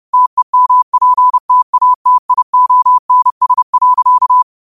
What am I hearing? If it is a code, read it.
Morse code, heard as a keyed tone, about 30 wpm, 1000 Hz. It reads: TEMPTATIONS.
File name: qotdfiles / 30wpm